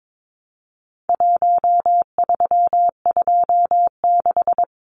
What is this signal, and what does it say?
Morse code, heard as a keyed tone, about 22 wpm, 700 Hz. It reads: 1326
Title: Numbers - Contest Style - 4-Digit 22wpm